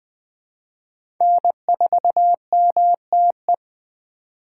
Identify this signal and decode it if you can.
Morse code, heard as a keyed tone, about 21 wpm, 700 Hz. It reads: N4MTE
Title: 2023 POTA Activator Call Signs - 20wpm